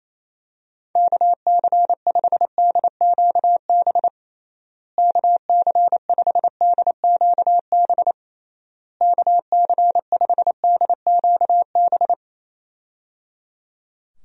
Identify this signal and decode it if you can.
Morse code, heard as a keyed tone, about 28 wpm, 700 Hz. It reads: KC5DQB KC5DQB KC5DQB
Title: USA Call Signs - 6 Characters - Repeated 3x - 28wpm